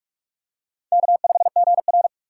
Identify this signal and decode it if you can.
Morse code, heard as a keyed tone, about 44 wpm, 700 Hz. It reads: K5GR
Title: USA Call Signs - 6x Character Spacing - 45wpm